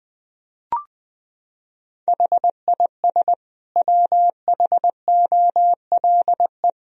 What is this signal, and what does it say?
Morse code, heard as a keyed tone, about 20 wpm, 700 Hz. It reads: HIS WHOLE
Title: Sentences from Top 1000 Words - Review Part 3 - 20wpm